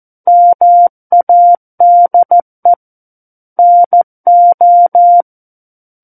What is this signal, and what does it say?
Morse code, heard as a keyed tone, about 14 wpm, 700 Hz. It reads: MADE NO
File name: Book One / 05-Chapter 5 - 14 WPM